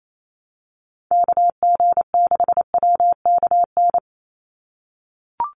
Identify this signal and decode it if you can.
Morse code, heard as a keyed tone, about 28 wpm, 700 Hz. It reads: KG6WKN
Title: USA Call Signs 28wpm